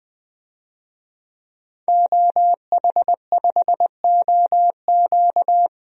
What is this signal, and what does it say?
Morse code, heard as a keyed tone, about 20 wpm, 700 Hz. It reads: OH5OQ
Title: DXCC-100 Call Signs - 5 Characters - Repeated 3x - 20wpm